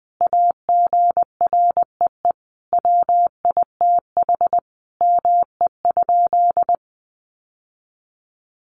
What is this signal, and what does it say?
Morse code, heard as a keyed tone, about 20 wpm, 700 Hz. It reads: AGREE WITH ME?
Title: Sentences from Top 1100 Words 20wpm